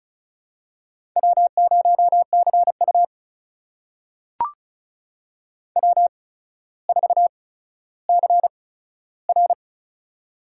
Morse code, heard as a keyed tone, about 36 wpm, spaced as for 15 wpm, 700 Hz. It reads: W0CU W4CR
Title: USA Call Signs - 4 Characters - 8x Character Spacing - 35wpm